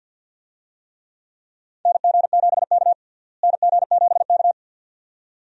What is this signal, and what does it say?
Morse code, heard as a keyed tone, about 48 wpm, 700 Hz. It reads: NG7K NG7K
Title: USA Call Signs - 4 Characters - Repeated 2x - 50wpm